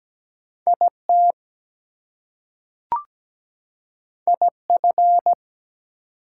Morse code, heard as a keyed tone, about 17 wpm, 700 Hz. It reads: IT IF
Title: Two Letter Words - 17wpm